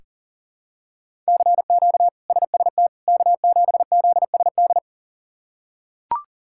Morse code, heard as a keyed tone, about 40 wpm, 700 Hz. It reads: CQ SST K7ZSD
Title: K1USN SST QSO Practice - 40wpm